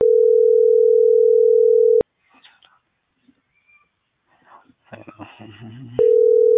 Ringback tone (North America), 440 + 480 Hz.